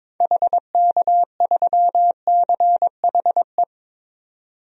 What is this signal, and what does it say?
Morse code, heard as a keyed tone, about 22 wpm, 700 Hz. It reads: HK3CHE
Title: DXCC-100 Call Signs - 6 Characters - 22wpm